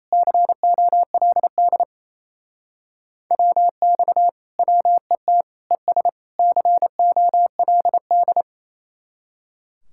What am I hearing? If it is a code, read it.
Morse code, heard as a keyed tone, about 35 wpm, 700 Hz. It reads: COLD WX WET ES COLD
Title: Speed-Racing Common QSO Phrases - 28wpm